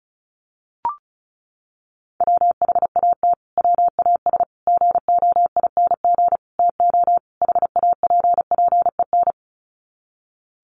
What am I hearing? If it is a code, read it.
Morse code, heard as a keyed tone, about 34 wpm, 700 Hz. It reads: WHAT WAS GOING TO HAPPEN
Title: Sets of 5 Words Encore 35wpm